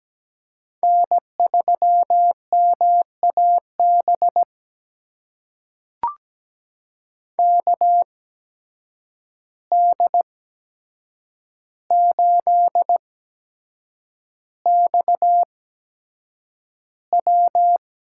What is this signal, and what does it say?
Morse code, heard as a keyed tone, about 17 wpm, spaced as for 7 wpm, 700 Hz. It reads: N3MAB KD8XW
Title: USA Call Signs - 5 Characters - 8x Character Spacing - 17wpm